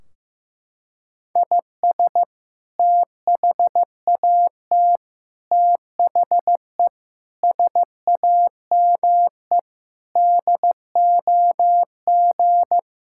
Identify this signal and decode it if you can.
Morse code, heard as a keyed tone, about 15 wpm, 700 Hz. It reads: IS THAT THE SAME DOG
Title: Sentences from Top 1600 Words 15wpm